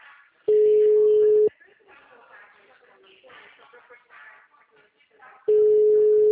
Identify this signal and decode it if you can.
Ringback tone (Europe, 425 Hz), with speech.
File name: out- -2045- - - .13064